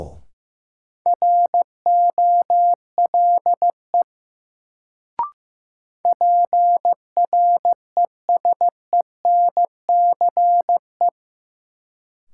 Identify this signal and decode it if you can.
Morse code, heard as a keyed tone, about 15 wpm, 700 Hz. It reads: ROLE PRESENCE
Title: Sets of 3 Words - New Word Drill - 15wpm